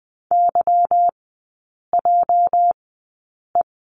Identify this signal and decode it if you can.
Morse code, heard as a keyed tone, about 20 wpm, spaced as for 8 wpm, 700 Hz. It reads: YJE
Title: 205 Lesson 33e - Letter j - Three Characters 20wpm